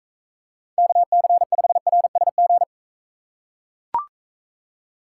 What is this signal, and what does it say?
Morse code, heard as a keyed tone, about 43 wpm, 700 Hz. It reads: KC5RSG